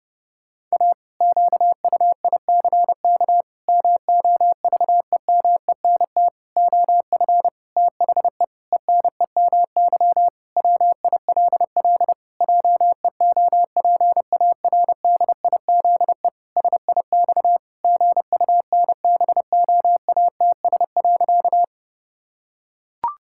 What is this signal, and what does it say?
Morse code, heard as a keyed tone, about 30 wpm, 700 Hz. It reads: A QUICK MOVEMENT OF THE ENEMY WILL JEOPARDIZE SIX GUNBOATS.
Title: Pangram Challenge v2 - 30wpm